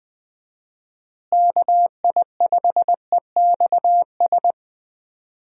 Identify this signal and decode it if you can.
Morse code, heard as a keyed tone, about 20 wpm, 700 Hz. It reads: KI5EXS
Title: USA Call Signs - 6 Characters - Repeated 3x - 20wpm